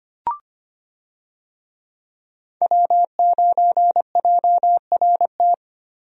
Morse code, heard as a keyed tone, about 25 wpm, 700 Hz. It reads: W9JRT
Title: USA Call Signs - 5 Characters - 25wpm